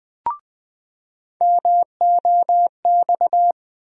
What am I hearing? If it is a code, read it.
Morse code, heard as a keyed tone, about 21 wpm, 700 Hz. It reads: MOX
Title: USA Call Sign Postfixes - 3 Letters 20wpm